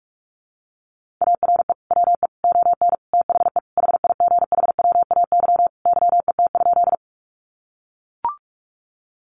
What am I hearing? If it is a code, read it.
Morse code, heard as a keyed tone, about 44 wpm, 700 Hz. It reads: ARE WE ON THE HIGHWAY YET?
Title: Sentences from Top 1400 Words 45wpm